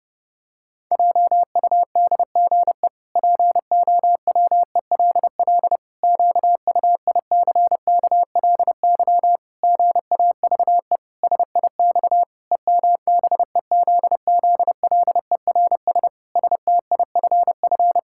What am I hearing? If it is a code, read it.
Morse code, heard as a keyed tone, about 30 wpm, 700 Hz. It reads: JUDGE POWELL QUICKLY GAVE SIX EMBEZZLERS STIFF
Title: Pangram Challenge v2 - 30wpm